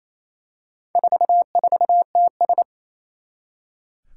Morse code, heard as a keyed tone, about 28 wpm, 700 Hz. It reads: 44TS
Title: Four Challenging Letter-Number Combinations 28wpm